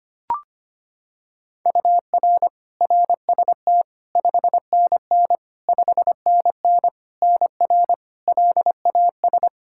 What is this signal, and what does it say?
Morse code, heard as a keyed tone, about 25 wpm, 700 Hz. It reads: UR RST 5NN 5NN NR LAS
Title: K1USN SST QSO Practice 25wpm